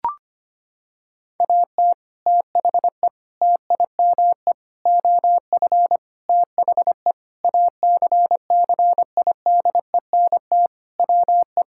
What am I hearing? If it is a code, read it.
Morse code, heard as a keyed tone, about 25 wpm, 700 Hz. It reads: AT THE TIME OF THE ACCIDENT WE
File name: practice / 25wpm / Sentences with 4-Word Phrases - 25wpm